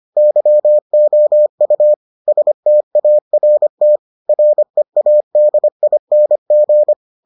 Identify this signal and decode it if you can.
Morse code, heard as a keyed tone, about 25 wpm, 600 Hz. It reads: YOU START READING